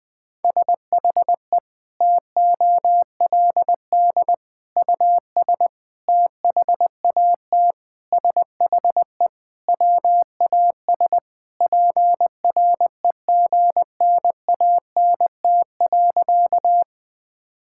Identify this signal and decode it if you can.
Morse code, heard as a keyed tone, about 20 wpm, 700 Hz. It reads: SHE TOLD US THAT SHE WAS PREGNANT.
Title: Sentences from Top 1000 Words - Review Part 6 - 20wpm